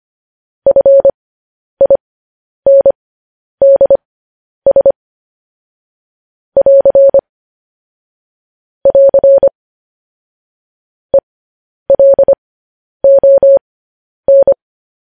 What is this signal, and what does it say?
Morse code, heard as a keyed tone, about 25 wpm, spaced as for 10 wpm, 550 Hz. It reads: FINDS <AR> <AR> ELON
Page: news headlines in morse code at 10 wpm